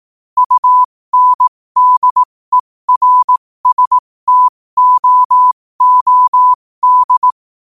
Morse code, heard as a keyed tone, about 18 wpm, spaced as for 15 wpm, 1000 Hz. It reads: UNDERSTOOD
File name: qotdfiles / 15wpm